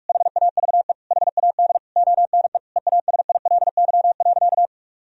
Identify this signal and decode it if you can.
Morse code, heard as a keyed tone, about 41 wpm, 700 Hz. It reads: HAVE HAD ONE EASILY.
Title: Sentences from Top 1000 Words - Review Part 3 - 45wpm